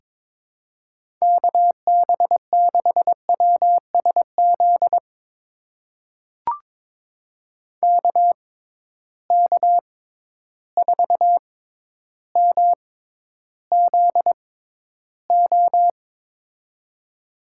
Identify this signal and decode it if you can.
Morse code, heard as a keyed tone, about 23 wpm, spaced as for 11 wpm, 700 Hz. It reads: KB6WSZ KK4MZO
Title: USA Call Signs - 6 Characters - 6x Character Spacing - 22wpm